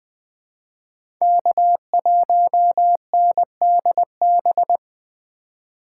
Morse code, heard as a keyed tone, about 21 wpm, 700 Hz. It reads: K1NDB
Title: USA Call Signs - 5 Characters - 3x Character Spacing - 20wpm